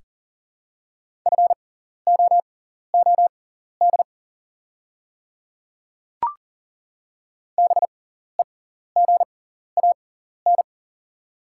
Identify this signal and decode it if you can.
Morse code, heard as a keyed tone, about 40 wpm, spaced as for 14 wpm, 690 Hz. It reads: FOOD BEGAN
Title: 6x Character Spacing - Top 300 Words - 40wpm